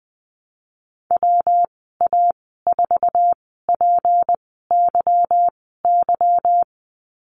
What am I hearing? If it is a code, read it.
Morse code, heard as a keyed tone, about 20 wpm, spaced as for 15 wpm, 700 Hz. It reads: WA4PYY
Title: 186 Lesson 30g - Letter b - Call Signs 20wpm